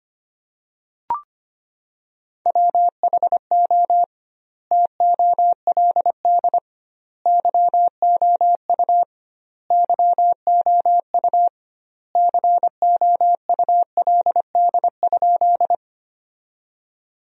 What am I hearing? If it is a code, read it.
Morse code, heard as a keyed tone, about 25 wpm, 700 Hz. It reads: WHO TOLD YOU YOU COULD?
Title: 2x Word-Spacing - Sentences from Top 400 Words - 25wpm